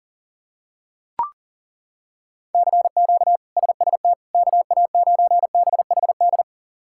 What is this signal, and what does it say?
Morse code, heard as a keyed tone, about 40 wpm, 700 Hz. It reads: CQ SST KA9BHD
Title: K1USN SST QSO Practice - 40wpm